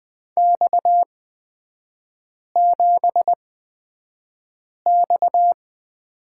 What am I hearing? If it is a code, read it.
Morse code, heard as a keyed tone, about 20 wpm, spaced as for 5 wpm, 700 Hz. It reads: X7X
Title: 250 Lesson 40e - BK - Three Characters 20wpm